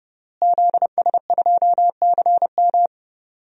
Morse code, heard as a keyed tone, about 30 wpm, 700 Hz. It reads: ZS2CM
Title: DXCC-100 Call Signs - 5 Characters - Repeated 2x - 30wpm